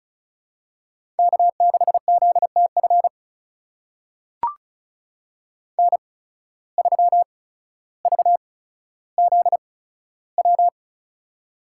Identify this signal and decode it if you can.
Morse code, heard as a keyed tone, about 36 wpm, spaced as for 15 wpm, 700 Hz. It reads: K6ZTF N3VZW
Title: USA Call Signs - 5 Characters - 8x Character Spacing - 35wpm